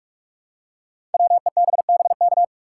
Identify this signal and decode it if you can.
Morse code, heard as a keyed tone, about 44 wpm, 700 Hz. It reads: WEBBK
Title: Code Groups - 5 Letter:Numbers - 45wpm